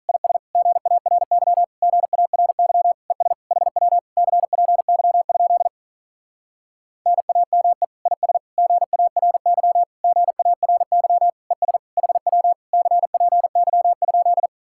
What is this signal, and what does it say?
Morse code, heard as a keyed tone, about 47 wpm, 700 Hz. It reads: IS GARY GARY ES HW CPY? NAME IS GARY GARY ES HW CPY?